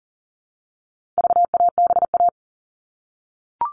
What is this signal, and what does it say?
Morse code, heard as a keyed tone, about 40 wpm, 700 Hz. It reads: VABA
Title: All Cut Numbers - 4-Digit 40wpm